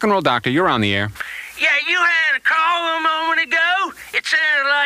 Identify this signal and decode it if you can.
Emergency Alert System (SAME) tones, heard as harsh buzzing bursts, with speech. It is heard from 1.1 to 3.7 s.